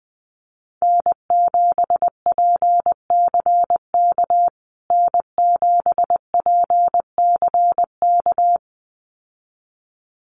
Morse code, heard as a keyed tone, about 20 wpm, 700 Hz. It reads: N7PCK N7PCK
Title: USA Call Signs Repeated 2x 20wpm